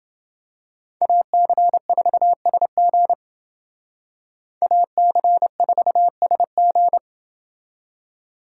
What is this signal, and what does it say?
Morse code, heard as a keyed tone, about 28 wpm, 700 Hz. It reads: AC4SG AC4SG